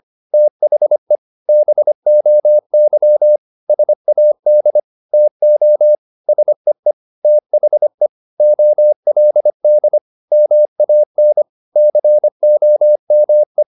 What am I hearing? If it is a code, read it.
Morse code, heard as a keyed tone, about 25 wpm, 600 Hz. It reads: THE BOY SAD TO SEE THE OLD MAN COME
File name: the old man and the sea part 1 25wpm